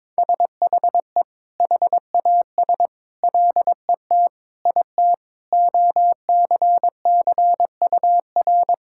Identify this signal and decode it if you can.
Morse code, heard as a keyed tone, about 22 wpm, 700 Hz. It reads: SHE HAS LET IT OCCUR
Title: Sentences from Top 1000 Words - Review Part 6 - 22wpm